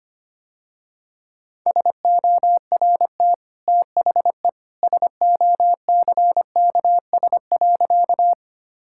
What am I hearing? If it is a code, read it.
Morse code, heard as a keyed tone, about 25 wpm, 700 Hz. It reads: SORT THE SOCKS.
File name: Easy Sentences with Hints - 25wpm